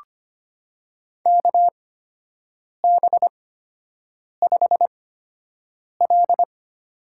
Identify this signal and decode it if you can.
Morse code, heard as a keyed tone, about 25 wpm, spaced as for 7 wpm, 700 Hz. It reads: KB5L
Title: USA Call Signs - 6 Characters - 8x Character Spacing - 25wpm